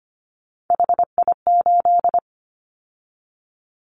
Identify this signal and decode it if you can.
Morse code, heard as a keyed tone, about 25 wpm, 700 Hz. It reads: HI8
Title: Callsign Prefixes for Top 100 DXCC - 25wpm